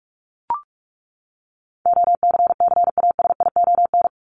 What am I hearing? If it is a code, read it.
Morse code, heard as a keyed tone, about 44 wpm, 700 Hz. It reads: OCCASION
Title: Top 1501-1750 Words 45wpm